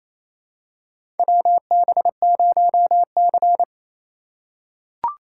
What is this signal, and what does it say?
Morse code, heard as a keyed tone, about 28 wpm, 700 Hz. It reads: WB0C
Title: USA Call Signs - 4 Characters - 3x Character Spacing - 28wpm